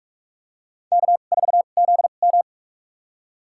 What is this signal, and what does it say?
Morse code, heard as a keyed tone, about 44 wpm, spaced as for 32 wpm, 700 Hz. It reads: K4ZM
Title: USA Call Signs - 4 Characters - 2x Character Spacing - 45wpm